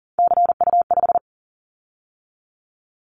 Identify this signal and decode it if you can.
Morse code, heard as a keyed tone, about 40 wpm, 700 Hz. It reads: CU5
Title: Callsign Prefixes for Top 100 DXCC - 40wpm